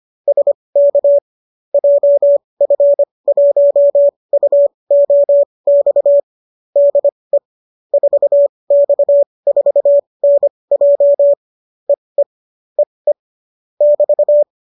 Morse code, heard as a keyed tone, about 25 wpm, spaced as for 20 wpm, 570 Hz. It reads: SK JF1UOX DE 4X4NJ EE EE <BT>